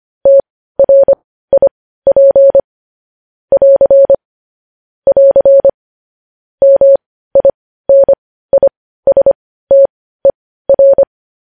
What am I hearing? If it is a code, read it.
Morse code, heard as a keyed tone, about 25 wpm, spaced as for 15 wpm, 550 Hz. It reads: TRIP <AR> <AR> MINISTER
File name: podcast / News headlines-1x-25x15